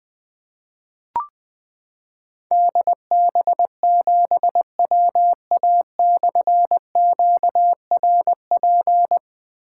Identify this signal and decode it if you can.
Morse code, heard as a keyed tone, about 20 wpm, 700 Hz. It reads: DB7WA/QRP
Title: Difficult Call Signs 20wpm